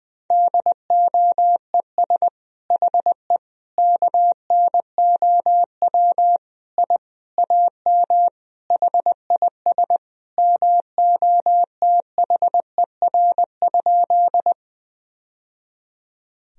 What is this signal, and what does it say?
Morse code, heard as a keyed tone, about 21 wpm, 700 Hz. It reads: DOES HE KNOW I AM HIS MOTHER?
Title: Sentences from Top 1000 Words - Review Part 2 - 20wpm